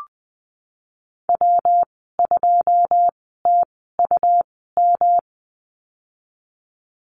Morse code, heard as a keyed tone, about 20 wpm, spaced as for 15 wpm, 700 Hz. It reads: W2TUM
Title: 179 Lesson 29g - Slash - Call Signs 20wpm